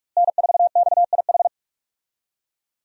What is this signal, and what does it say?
Morse code, heard as a keyed tone, about 42 wpm, 700 Hz. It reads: N4XIH
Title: USA Call Signs - 5 Characters - Repeated 2x - 45wpm